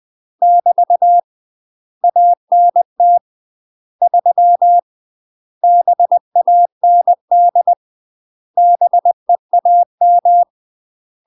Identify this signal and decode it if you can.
Morse code, heard as a keyed tone, about 20 wpm, 700 Hz. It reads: <BT> ANT 3 BAND BEAM
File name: UK MASTER QSOs TRAINING / 20WPM / UK MASTER QSOs 0067